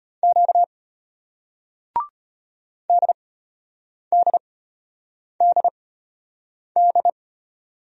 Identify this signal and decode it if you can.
Morse code, heard as a keyed tone, about 33 wpm, spaced as for 8 wpm, 700 Hz. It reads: Q DDDD